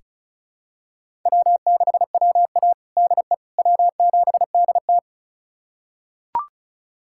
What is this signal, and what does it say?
Morse code, heard as a keyed tone, about 36 wpm, 700 Hz. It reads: W6WA DE W7DT
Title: USA Callsign DE Callsign - 4 Characters - 35wpm